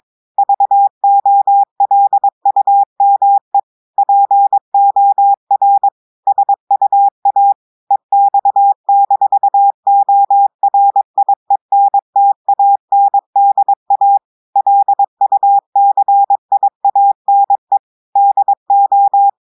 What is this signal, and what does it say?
Morse code, heard as a keyed tone, about 22 wpm, 800 Hz. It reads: VOLUME POR SUA EX-ORIENTANDA LUCIANE DO
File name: audios / portugais / portugais texte en clair 1 vitesse 22 mots minutes